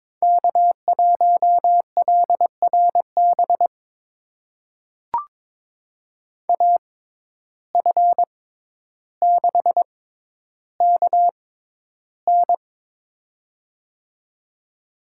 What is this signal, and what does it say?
Morse code, heard as a keyed tone, about 22 wpm, spaced as for 11 wpm, 700 Hz. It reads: K1LRB AF6KN
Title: USA Call Signs - 5 Characters - 6x Character Spacing - 22wpm